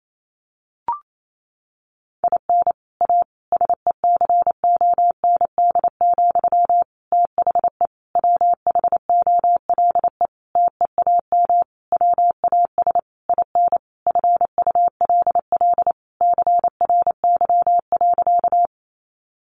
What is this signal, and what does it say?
Morse code, heard as a keyed tone, about 28 wpm, 700 Hz. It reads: IN A SECOND, THE WHOLE TEAM WAS IN FULL CRY.
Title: Sentences from Top 1500 Words 28wpm